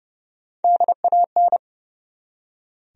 Morse code, heard as a keyed tone, about 32 wpm, 700 Hz. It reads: DAN
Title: USA Common Names - Max 5 Letters - 30wpm